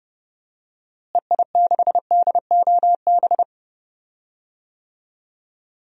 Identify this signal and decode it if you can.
Morse code, heard as a keyed tone, about 30 wpm, 700 Hz. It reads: EI6DOB
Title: DXCC-100 Call Signs - 6 Characters - Repeated 3x - 30wpm